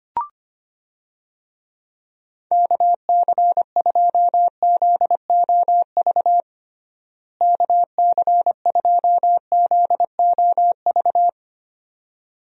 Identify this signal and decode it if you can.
Morse code, heard as a keyed tone, about 25 wpm, 700 Hz. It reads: KC2ZOV KC2ZOV
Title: USA Call Signs - 6 Characters - Repeated 2x - 25wpm